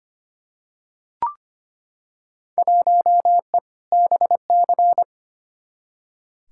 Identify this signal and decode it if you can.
Morse code, heard as a keyed tone, about 25 wpm, 700 Hz. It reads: 1E BC
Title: Field Day Exchanges - 25wpm